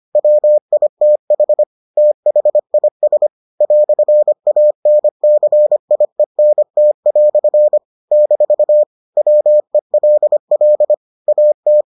Morse code, heard as a keyed tone, about 25 wpm, 600 Hz. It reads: WITH THIS "ANCIENT" - WELL AT